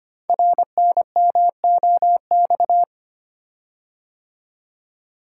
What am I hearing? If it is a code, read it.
Morse code, heard as a keyed tone, about 25 wpm, 700 Hz. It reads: RNMOX
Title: Code Groups - 5 Letters - 25wpm